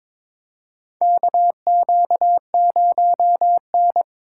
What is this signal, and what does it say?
Morse code, heard as a keyed tone, about 22 wpm, 700 Hz. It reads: KQ0N
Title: USA Call Signs - 6x Character Spacing - 22wpm